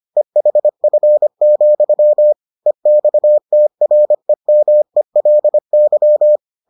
Morse code, heard as a keyed tone, about 25 wpm, 600 Hz. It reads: EHF, EXTREMELY